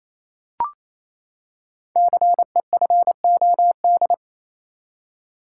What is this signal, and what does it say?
Morse code, heard as a keyed tone, about 28 wpm, 700 Hz. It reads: CEFOD
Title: Code Groups - 5 Letters - 28wpm